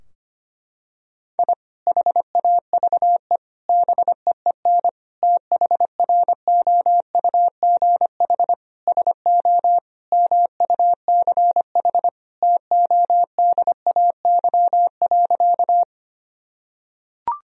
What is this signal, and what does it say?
Morse code, heard as a keyed tone, about 25 wpm, 700 Hz. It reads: I HAVE BEEN THROUGH SO MUCH TODAY.
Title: Sentences from Top 1000 Words - Review Part 4 - 25wpm